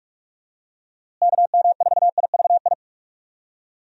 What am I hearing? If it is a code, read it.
Morse code, heard as a keyed tone, about 46 wpm, 700 Hz. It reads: KM4IVI
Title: USA Call Signs - 6 Characters - Repeated 2x - 45wpm